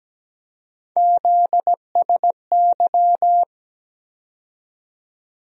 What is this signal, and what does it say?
Morse code, heard as a keyed tone, about 17 wpm, 700 Hz. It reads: ZSY
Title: USA Call Sign Postfixes - 3 Letters 17wpm